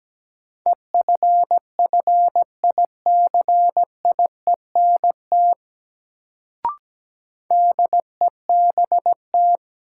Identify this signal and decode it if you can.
Morse code, heard as a keyed tone, about 17 wpm, 700 Hz. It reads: EFFICIENT DEBT
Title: Sentences from Top 1300 Words - New Word Drill - 17wpm